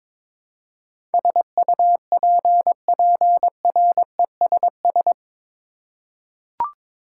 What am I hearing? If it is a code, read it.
Morse code, heard as a keyed tone, about 22 wpm, 700 Hz. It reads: SUPPRESS
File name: practice / 22wpm / Eight Letter Words - 22wpm